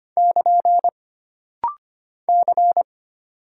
Morse code, heard as a keyed tone, about 25 wpm, 700 Hz. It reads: <KN> C
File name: Single Character (Letter:Number:Punctuation:Prosign) - Rapid-Fire - 25wpm